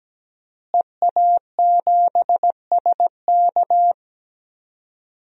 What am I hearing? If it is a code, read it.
Morse code, heard as a keyed tone, about 17 wpm, 700 Hz. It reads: EA7SK
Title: DXCC-100 Call Signs - 5 Characters - Repeated 3x - 17wpm